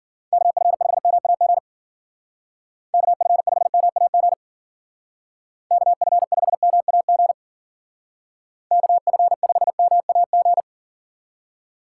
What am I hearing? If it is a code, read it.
Morse code, heard as a keyed tone, about 50 wpm, 700 Hz. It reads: KF5MAG KF5MAG KF5MAG KF5MAG
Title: Speed Racing - USA Call Signs 40wpm